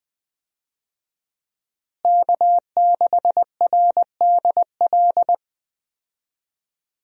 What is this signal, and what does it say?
Morse code, heard as a keyed tone, about 20 wpm, 700 Hz. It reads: K6RDL
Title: USA Call Signs - 5 Characters - Repeated 3x - 20wpm